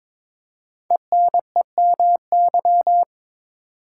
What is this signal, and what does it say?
Morse code, heard as a keyed tone, about 22 wpm, 700 Hz. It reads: ENEMY
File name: 5 Letter Words - First 3 Letters Given - 22wpm